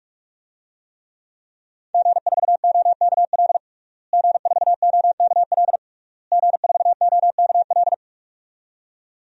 Morse code, heard as a keyed tone, about 46 wpm, 700 Hz. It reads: G4OKL G4OKL G4OKL
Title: DXCC-100 Call Signs - 5 Characters - Repeated 3x - 45wpm